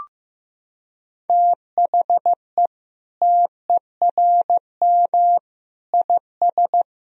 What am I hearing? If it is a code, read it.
Morse code, heard as a keyed tone, about 16 wpm, 700 Hz. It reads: THE TERM IS
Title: Sentences from Top 1000 Words - Review Part 4 - 15wpm